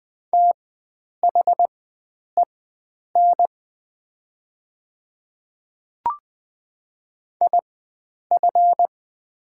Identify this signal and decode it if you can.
Morse code, heard as a keyed tone, about 20 wpm, spaced as for 9 wpm, 700 Hz. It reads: THEN IF
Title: 4x Character Spacing - Top 100 Words - 20wpm